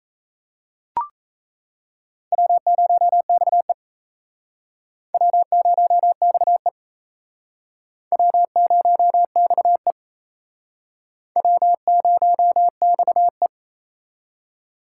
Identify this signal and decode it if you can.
Morse code, heard as a keyed tone, about 38 wpm, 700 Hz. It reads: W0XE W0XE W0XE W0XE